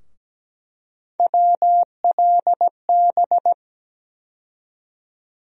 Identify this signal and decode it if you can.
Morse code, heard as a keyed tone, about 17 wpm, 700 Hz. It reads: WLB
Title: USA Call Sign Postfixes - 3 Letters 17wpm